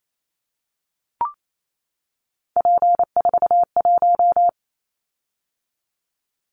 Morse code, heard as a keyed tone, about 28 wpm, 700 Hz. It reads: P41
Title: Callsign Prefixes for Top 100 DXCC - 28wpm